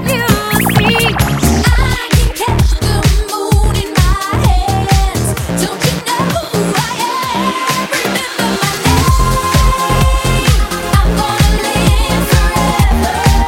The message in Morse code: none